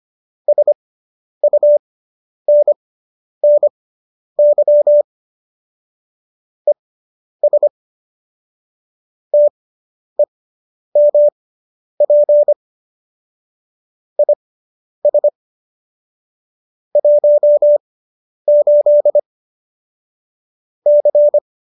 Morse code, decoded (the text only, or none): SUNNY ES TEMP IS 18 C